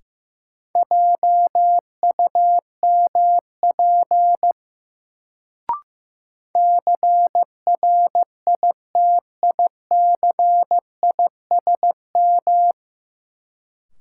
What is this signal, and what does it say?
Morse code, heard as a keyed tone, about 15 wpm, 700 Hz. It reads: JUMP CRITICISM
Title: Sentences from Top 1300 Words - New Word Drill - 15wpm